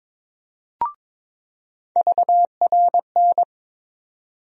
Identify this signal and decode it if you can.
Morse code, heard as a keyed tone, about 22 wpm, 700 Hz. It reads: VRN
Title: USA Call Sign Postfixes - 3 Letters 22wpm